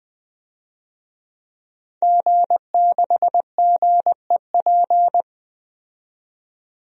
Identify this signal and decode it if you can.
Morse code, heard as a keyed tone, about 20 wpm, 700 Hz. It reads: G6GEP
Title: DXCC-100 Call Signs - 5 Characters - 20wpm